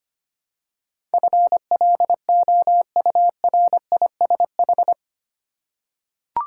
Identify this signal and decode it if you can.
Morse code, heard as a keyed tone, about 25 wpm, 700 Hz. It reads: FLOURISH
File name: 8 Letter Words - First 3 Letters Given - 25wpm